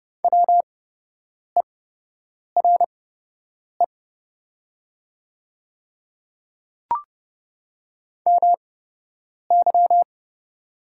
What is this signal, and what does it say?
Morse code, heard as a keyed tone, about 30 wpm, spaced as for 8 wpm, 700 Hz. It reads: WERE MY
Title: 8x Character Spacing - Top 100 Words - 30wpm